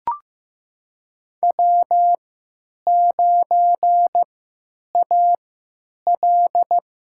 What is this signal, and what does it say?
Morse code, heard as a keyed tone, about 15 wpm, spaced as for 9 wpm, 700 Hz. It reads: W9AL
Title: USA Call Signs - 4 Characters - 3x Character Spacing - 15wpm